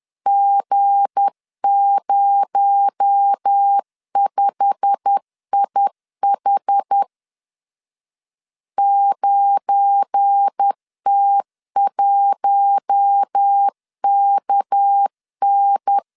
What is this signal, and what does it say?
Morse code, heard as a keyed tone, about 10 wpm, 800 Hz. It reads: G05IH 9T1KN
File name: LEZIONI / 9 2867 10wpm 2733